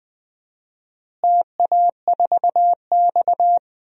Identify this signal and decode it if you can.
Morse code, heard as a keyed tone, about 20 wpm, 700 Hz. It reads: TA4X
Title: DXCC-100 Call Signs - 4 Characters - Repeated 2x - 20wpm